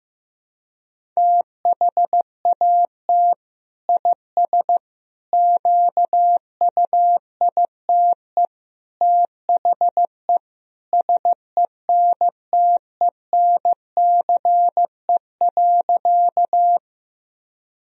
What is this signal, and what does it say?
Morse code, heard as a keyed tone, about 16 wpm, 700 Hz. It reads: THAT IS QUITE THE SENTENCE.
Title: Sentences from Top 1000 Words - Review Part 1 - 15wpm